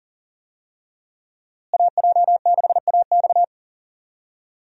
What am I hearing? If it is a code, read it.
Morse code, heard as a keyed tone, about 37 wpm, 700 Hz. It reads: AJ6AX
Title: USA Call Signs - 5 Characters - Repeated 3x - 40wpm